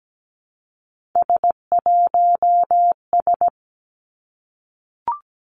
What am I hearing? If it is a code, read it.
Morse code, heard as a keyed tone, about 17 wpm, 700 Hz. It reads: S1S
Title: Three Challenging Letter-Number Combinations 17wpm